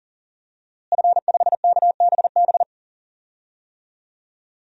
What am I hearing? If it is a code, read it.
Morse code, heard as a keyed tone, about 37 wpm, 700 Hz. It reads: F5KBB
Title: DXCC-100 Call Signs - 5 Characters - Repeated 3x - 40wpm